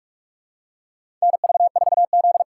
Morse code, heard as a keyed tone, about 46 wpm, 700 Hz. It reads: NV4Z
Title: USA Call Signs - 3x Character Spacing - 45wpm